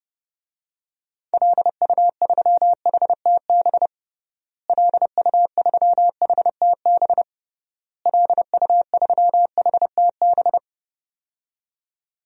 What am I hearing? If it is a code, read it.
Morse code, heard as a keyed tone, about 30 wpm, 700 Hz. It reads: LU3HTB LU3HTB LU3HTB
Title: DXCC-100 Call Signs - 6 Characters - Repeated 3x - 30wpm